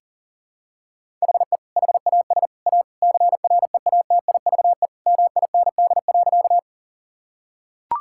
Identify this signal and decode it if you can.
Morse code, heard as a keyed tone, about 40 wpm, 700 Hz. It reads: HE HAS A CREATIVE MIND.
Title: Sentences from Top 1000 Words - Review Part 5 - 40wpm